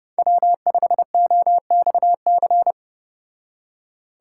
Morse code, heard as a keyed tone, about 30 wpm, 700 Hz. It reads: W5OXC